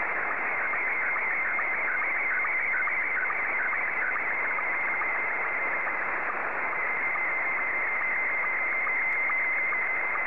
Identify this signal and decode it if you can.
SSTV (slow-scan television), Scottie 1.
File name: sstv / S1-JL1FQZ